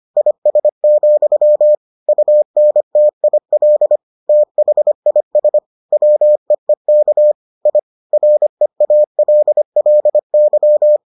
Morse code, heard as a keyed tone, about 25 wpm, 600 Hz. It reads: IS, UNTIL THIS WEEK I REALLY